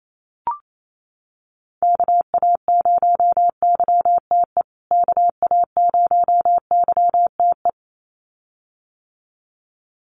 Morse code, heard as a keyed tone, about 28 wpm, 700 Hz. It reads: KA0YTE KA0YTE
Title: USA Call Signs Repeated 2x 28wpm